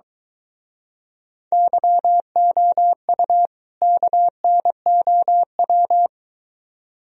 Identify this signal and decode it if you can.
Morse code, heard as a keyed tone, about 23 wpm, 700 Hz. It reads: YOU KNOW